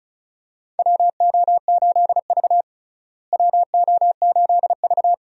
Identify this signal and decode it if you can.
Morse code, heard as a keyed tone, about 36 wpm, 700 Hz. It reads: WO8V WO8V
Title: USA Call Signs - 4 Characters - Repeated 2x - 35wpm